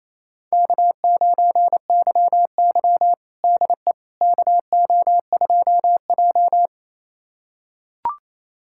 Morse code, heard as a keyed tone, about 28 wpm, 700 Hz. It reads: K9YY DE KO2J
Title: USA Callsign DE Callsign - 4 Characters - 28wpm